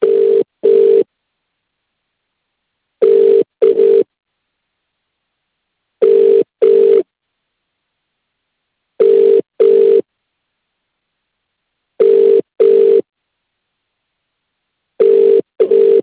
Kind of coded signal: telephone call progress tone